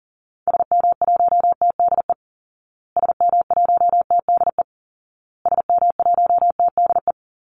Morse code, heard as a keyed tone, about 40 wpm, 700 Hz. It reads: SM1TDE SM1TDE SM1TDE
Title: Call Signs Repeated 2x - 40wpm